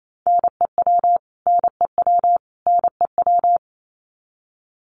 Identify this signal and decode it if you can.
Morse code, heard as a keyed tone, about 28 wpm, 700 Hz. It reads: NEW NEW NEW
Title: Top 200 Words Repeated 28wpm